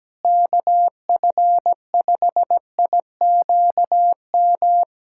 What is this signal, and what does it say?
Morse code, heard as a keyed tone, about 17 wpm, 700 Hz. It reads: KF5IQM
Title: USA Callsign DE Callsign 17wpm